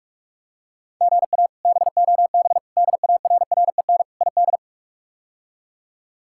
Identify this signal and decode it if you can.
Morse code, heard as a keyed tone, about 42 wpm, 700 Hz. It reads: GA BOB DARREN ID
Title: K1USN SST QSO Practice - 45wpm